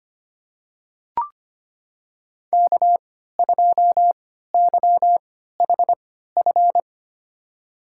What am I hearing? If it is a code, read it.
Morse code, heard as a keyed tone, about 25 wpm, spaced as for 14 wpm, 700 Hz. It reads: K2YHF
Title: USA Call Signs - 5 Characters - 3x Character Spacing - 25wpm